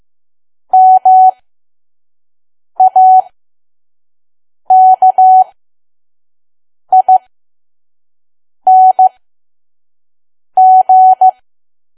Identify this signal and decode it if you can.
Morse code, heard as a keyed tone, about 15 wpm, spaced as for 5 wpm, 750 Hz. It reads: MAKING
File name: Archive / 5 WPM files / PM